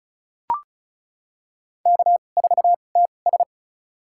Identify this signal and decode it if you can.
Morse code, heard as a keyed tone, about 36 wpm, spaced as for 25 wpm, 700 Hz. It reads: K4TS
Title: USA Call Signs - 4 Characters - 2x Character Spacing - 35wpm